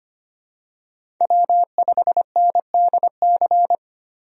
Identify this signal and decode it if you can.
Morse code, heard as a keyed tone, about 25 wpm, 700 Hz. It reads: W5NDC
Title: USA Call Signs - 5 Characters - Repeated 3x - 25wpm